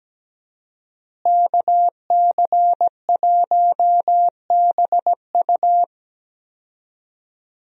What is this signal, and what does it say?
Morse code, heard as a keyed tone, about 17 wpm, 700 Hz. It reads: KC1BU
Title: USA Call Signs - 5 Characters - Repeated 3x - 17wpm